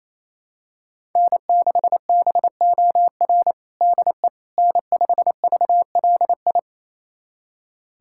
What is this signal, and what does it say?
Morse code, heard as a keyed tone, about 28 wpm, 700 Hz. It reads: N6BOR DE N5VLI
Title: USA Callsign DE Callsign - 5 Characters - 28wpm